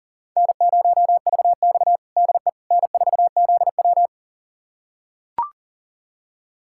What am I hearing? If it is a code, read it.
Morse code, heard as a keyed tone, about 40 wpm, 700 Hz. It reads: N0VX DE N4ZW
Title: USA Callsign DE Callsign - 4 Characters - 40wpm